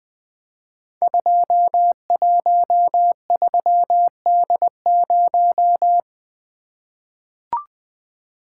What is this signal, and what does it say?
Morse code, heard as a keyed tone, about 20 wpm, 700 Hz. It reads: 213D0
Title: Code Groups - 5 Letter:Numbers - 20wpm